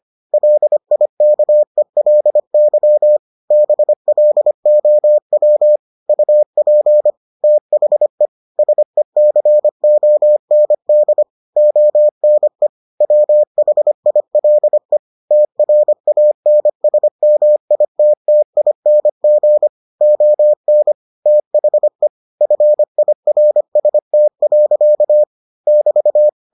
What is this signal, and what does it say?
Morse code, heard as a keyed tone, about 25 wpm, 600 Hz. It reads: LIKELY BLOW UP THE SECOND ONE WHILE TRANSMITTING ON THE FIRST. <BT>